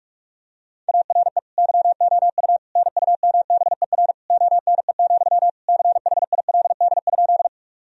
Morse code, heard as a keyed tone, about 43 wpm, 700 Hz. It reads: ARE YOU NUMBER ONE, CHILD?
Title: Sentences from Top 1000 Words - Review Part 1 - 45wpm